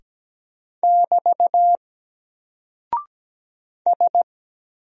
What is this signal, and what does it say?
Morse code, heard as a keyed tone, about 17 wpm, 700 Hz. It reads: <BT> S
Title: Single Character (Letter:Number:Punctuation:Prosign) 17wpm